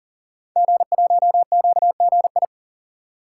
Morse code, heard as a keyed tone, about 39 wpm, 700 Hz. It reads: G1QGI
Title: Code Groups - 5 Letter:Numbers - 40wpm